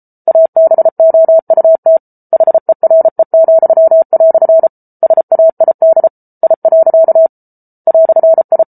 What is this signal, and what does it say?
Morse code, heard as a keyed tone, about 34 wpm, 650 Hz. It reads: ABOUT HERE," SAID I. "I